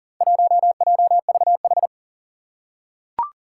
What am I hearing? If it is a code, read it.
Morse code, heard as a keyed tone, about 40 wpm, 700 Hz. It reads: 1JVH
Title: Four Challenging Letter-Number Combinations 40wpm